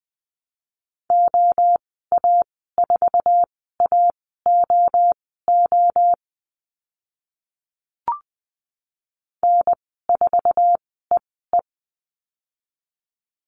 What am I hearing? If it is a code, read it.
Morse code, heard as a keyed tone, about 20 wpm, spaced as for 15 wpm, 700 Hz. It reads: OA4AOO N4EE
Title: 062 Lesson 11g - Letter h - Call Signs 20wpm